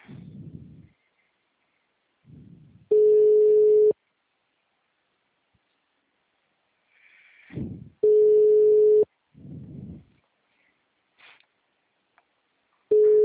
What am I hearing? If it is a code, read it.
Ringback tone (Europe, 425 Hz).
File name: out- -2001- - - .34391